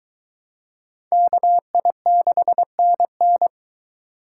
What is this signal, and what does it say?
Morse code, heard as a keyed tone, about 22 wpm, 700 Hz. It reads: KI6NN